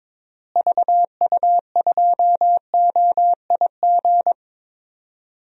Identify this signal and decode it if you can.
Morse code, heard as a keyed tone, about 22 wpm, 700 Hz. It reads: VU2OIG
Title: DXCC-100 Call Signs - 6 Characters - Repeated 2x - 22wpm